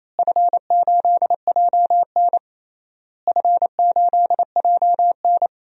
Morse code, heard as a keyed tone, about 28 wpm, 700 Hz. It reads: F8JN F8JN
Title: DXCC-100 Call Signs - 4 Characters - Repeated 3x - 28wpm